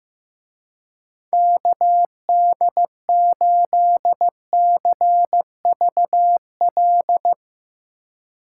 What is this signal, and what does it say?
Morse code, heard as a keyed tone, about 16 wpm, 700 Hz. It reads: KD8CVL
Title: Speed Racing - USA 2x3 Call Signs 15wpm